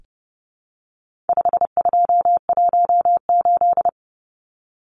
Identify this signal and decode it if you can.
Morse code, heard as a keyed tone, about 30 wpm, 700 Hz. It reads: 5218
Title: Numbers - 4-Digit 30wpm